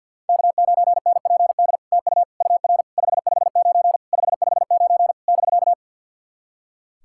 Morse code, heard as a keyed tone, about 52 wpm, 700 Hz. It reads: K9NPD TU UR 559 559 <BK>
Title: SOTA QSO Practice 50wpm